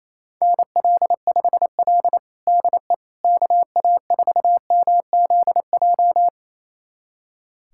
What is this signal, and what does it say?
Morse code, heard as a keyed tone, about 28 wpm, 700 Hz. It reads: NL5L DE KA4MZJ
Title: USA Callsign DE Callsign 28wpm